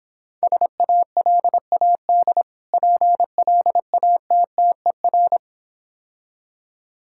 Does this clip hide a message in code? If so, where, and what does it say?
Yes: Morse code from 0.4 to 5.4 s, heard as a keyed tone, about 26 wpm, 690 Hz. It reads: SALAD PLATTER